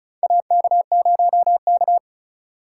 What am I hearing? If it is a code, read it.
Morse code, heard as a keyed tone, about 36 wpm, 700 Hz. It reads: AK0K
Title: USA Call Signs - 4 Characters - 3x Character Spacing - 35wpm